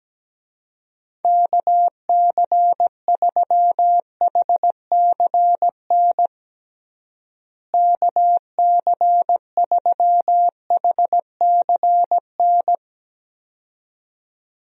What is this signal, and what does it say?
Morse code, heard as a keyed tone, about 17 wpm, 700 Hz. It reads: KC3HCN KC3HCN
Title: USA Call Signs - 6 Characters - Repeated 2x - 17wpm